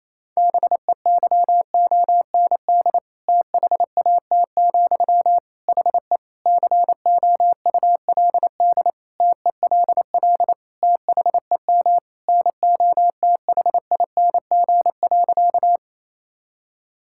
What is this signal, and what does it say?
Morse code, heard as a keyed tone, about 28 wpm, 700 Hz. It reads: BEYOND THAT, HE COULD TELL THEM NOTHING.
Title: Sentences from Top 1000 Words - Review Part 6 - 28wpm